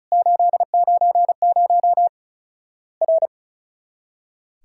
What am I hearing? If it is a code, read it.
Morse code, heard as a keyed tone, about 36 wpm, 700 Hz. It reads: 890 R
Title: Worked All Europe DX Contest - QTC Exchange - 35wpm